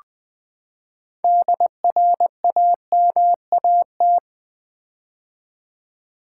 Morse code, heard as a keyed tone, about 20 wpm, 700 Hz. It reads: DRAMAT
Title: Build A Word - 6 Letters At A Time - 12-Letter Words - 20wpm